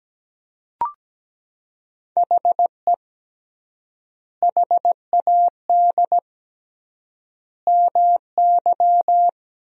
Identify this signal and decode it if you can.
Morse code, heard as a keyed tone, about 17 wpm, 700 Hz. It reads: HE HAD MY
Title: 3x Word-Spacing - Sentences from Top 400 Words - 17wpm